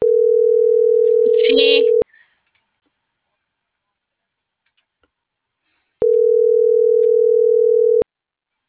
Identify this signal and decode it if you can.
Ringback tone (North America), 440 + 480 Hz.